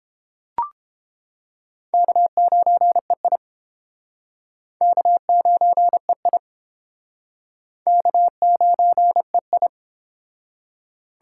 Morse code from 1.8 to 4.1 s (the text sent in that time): K9EI